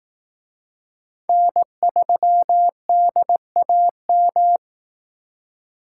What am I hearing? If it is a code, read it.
Morse code, heard as a keyed tone, about 18 wpm, 700 Hz. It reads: N3DAM
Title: Speed Racing - USA 1x3 Call Signs 15wpm